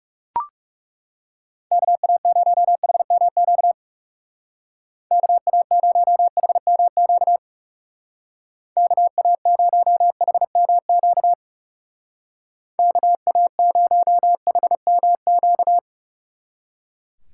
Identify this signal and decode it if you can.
Morse code, heard as a keyed tone, about 38 wpm, 700 Hz. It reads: KA0HMQ KA0HMQ KA0HMQ KA0HMQ